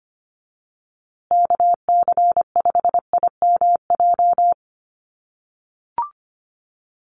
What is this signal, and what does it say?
Morse code, heard as a keyed tone, about 25 wpm, 700 Hz. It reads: KC5IMJ
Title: USA Call Signs Repeated 3x 25wpm